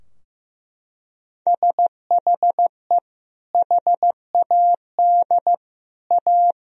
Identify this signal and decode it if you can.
Morse code, heard as a keyed tone, about 16 wpm, 700 Hz. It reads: SHE HAD A
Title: Sentences from Top 1000 Words - Review Part 2 - 15wpm